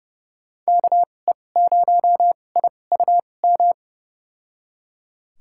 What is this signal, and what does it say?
Morse code, heard as a keyed tone, about 30 wpm, spaced as for 22 wpm, 700 Hz. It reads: KE0IUM
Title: USA Call Signs - 6 Characters - 2x Character Spacing - 30wpm